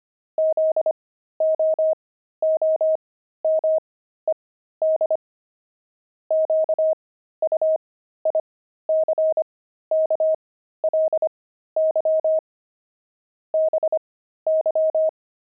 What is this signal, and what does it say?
Morse code, heard as a keyed tone, about 25 wpm, spaced as for 13 wpm, 630 Hz. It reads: ZOOMED QUICKLY BY
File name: pangram-fs-13-cs-25-jet